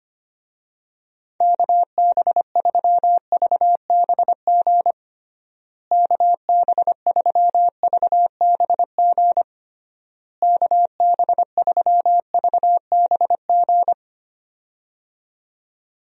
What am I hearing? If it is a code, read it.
Morse code, heard as a keyed tone, about 25 wpm, 700 Hz. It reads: KB3VBG KB3VBG KB3VBG
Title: USA Call Signs - 6 Characters - Repeated 3x - 25wpm